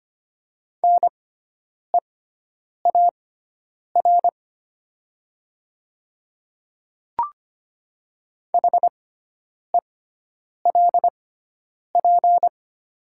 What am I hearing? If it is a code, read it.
Morse code, heard as a keyed tone, about 25 wpm, spaced as for 9 wpm, 700 Hz. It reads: NEAR HELP
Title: 6x Character Spacing - Top 200 Words - 25wpm